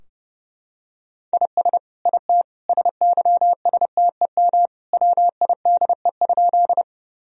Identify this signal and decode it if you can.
Morse code, heard as a keyed tone, about 30 wpm, 700 Hz. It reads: IS IT SYSTEM WIDE?
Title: Sentences from Top 1000 Words - Review Part 3 - 30wpm